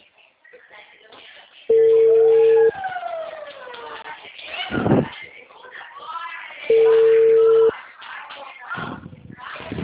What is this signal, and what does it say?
Ringback tone (China, 450 Hz), with speech.